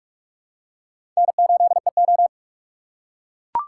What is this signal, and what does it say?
Morse code, heard as a keyed tone, about 42 wpm, 700 Hz. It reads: N8EO
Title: USA Call Signs - 4 Characters - 3x Character Spacing - 45wpm